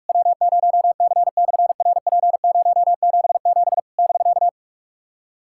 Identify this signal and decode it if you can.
Morse code, heard as a keyed tone, about 46 wpm, 700 Hz. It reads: W0C/RP077 <BK>
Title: SOTA QSO Practice 45wpm